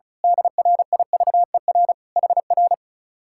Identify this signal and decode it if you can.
Morse code, heard as a keyed tone, about 32 wpm, 700 Hz. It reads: DRIVER HR
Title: Speed-Racing Common QSO Phrases - 35wpm